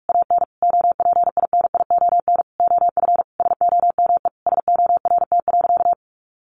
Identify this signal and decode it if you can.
Morse code, heard as a keyed tone, about 44 wpm, 700 Hz. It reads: AN OPINION OF SOME SORT.
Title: Sentences from Top 1300 Words 45wpm